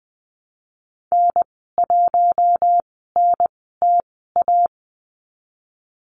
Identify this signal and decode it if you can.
Morse code, heard as a keyed tone, about 19 wpm, spaced as for 14 wpm, 700 Hz. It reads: N1NTA
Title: 048 Lesson 9g - Number 4 - Call Signs 20wpm